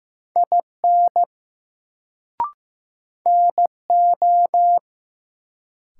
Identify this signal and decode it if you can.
Morse code, heard as a keyed tone, about 15 wpm, 700 Hz. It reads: IN NO
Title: Sentences from Top 100 Words - New Word Drill - 15wpm